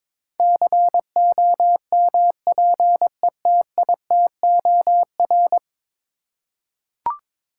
Morse code, heard as a keyed tone, about 22 wpm, 700 Hz. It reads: COMPETITOR
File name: practice / 22wpm / 10 Letter Words - First 3 Letters Given - 22wpm